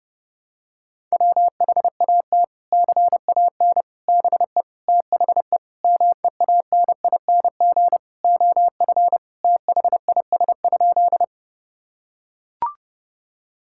Morse code, heard as a keyed tone, about 30 wpm, 700 Hz. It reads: WHAT CAN BE THE MEANING OF THIS?
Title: Sentences from Top 1000 Words - Review Part 6 - 30wpm